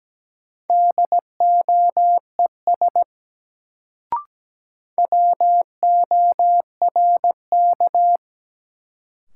Morse code, heard as a keyed tone, about 17 wpm, 700 Hz. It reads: DOES WORK
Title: Sentences from Top 200 Words - New Word Drill - 17wpm